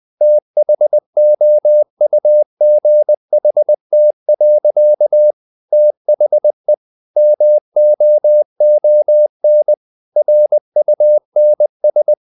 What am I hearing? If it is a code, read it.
Morse code, heard as a keyed tone, about 20 wpm, 600 Hz. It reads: THOUGHT. THE MOON RUNS
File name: the old man and the sea part 2 20wpm